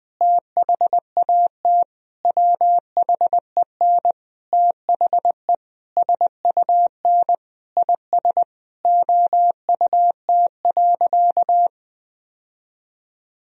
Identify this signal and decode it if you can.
Morse code, heard as a keyed tone, about 21 wpm, 700 Hz. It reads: THAT WHEN THE SUN IS OUT.
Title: Sentences from Top 1000 Words - Review Part 1 - 20wpm